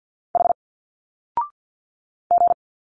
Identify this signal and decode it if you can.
Morse code, heard as a keyed tone, about 49 wpm, 700 Hz. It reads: H G
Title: Single Letters 50wpm